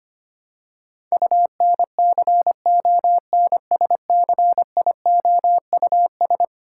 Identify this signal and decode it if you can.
Morse code, heard as a keyed tone, about 25 wpm, 700 Hz. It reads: UNCONSCIOUS
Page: morse-code-ninja-practice-25wpm